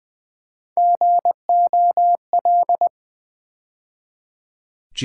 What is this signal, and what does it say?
Morse code, heard as a keyed tone, about 20 wpm, 700 Hz. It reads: GOL
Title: Build A Word - 3 Letters At A Time - 6-Letter Words - 20wpm